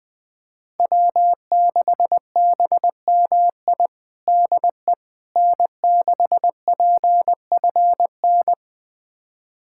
Morse code, heard as a keyed tone, about 21 wpm, 700 Hz. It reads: W6BMI DE N6PFN
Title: USA Callsign DE Callsign - 5 Characters - 20wpm